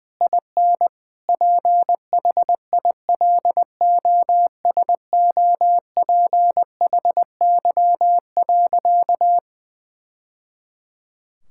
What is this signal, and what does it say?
Morse code, heard as a keyed tone, about 21 wpm, 700 Hz. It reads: IN PHILOSOPHY.
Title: Sentences from Top 1000 Words - Review Part 6 - 20wpm